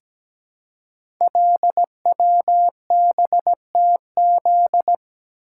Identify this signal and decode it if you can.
Morse code, heard as a keyed tone, about 17 wpm, 700 Hz. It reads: LWBTZ
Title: Code Groups - 5 Letter:Numbers - 17wpm